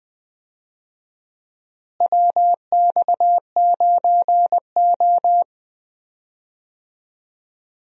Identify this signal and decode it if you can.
Morse code, heard as a keyed tone, about 20 wpm, 700 Hz. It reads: WX9O
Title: USA Call Signs - 4 Characters - Repeated 2x - 20wpm